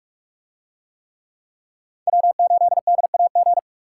Morse code, heard as a keyed tone, about 44 wpm, 700 Hz. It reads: W8DAG